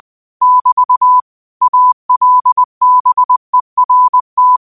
Morse code, heard as a keyed tone, about 20 wpm, 1000 Hz. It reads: <BT> ALBERT
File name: qotdfiles / 20wpm